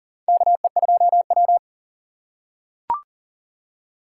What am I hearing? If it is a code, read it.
Morse code, heard as a keyed tone, about 40 wpm, 700 Hz. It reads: KE2W
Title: USA Call Signs - 4 Characters - 3x Character Spacing - 40wpm